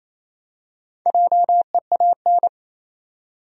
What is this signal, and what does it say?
Morse code, heard as a keyed tone, about 28 wpm, 700 Hz. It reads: JEAN
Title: USA Common Names - Max 5 Letters - 28wpm